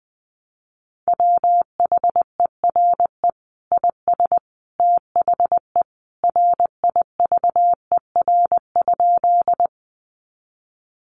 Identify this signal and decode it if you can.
Morse code, heard as a keyed tone, about 20 wpm, 700 Hz. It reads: WHERE IS THE RIVER?
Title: Sentences from Top 300 Words 20wpm